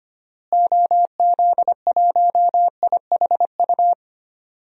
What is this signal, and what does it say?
Morse code, heard as a keyed tone, about 25 wpm, 700 Hz. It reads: OZ1IHU
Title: DXCC-100 Call Signs - 6 Characters - Repeated 2x - 25wpm